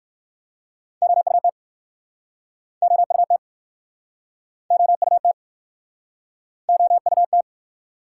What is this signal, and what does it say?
Morse code, heard as a keyed tone, about 61 wpm, 700 Hz. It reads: OUT OUT OUT OUT